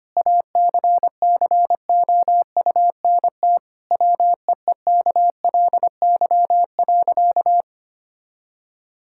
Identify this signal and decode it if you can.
Morse code, heard as a keyed tone, about 25 wpm, 700 Hz. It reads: ACCOUNT WEEKLY.
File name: practice / 25wpm / Difficult Sentences with Hints - 25wpm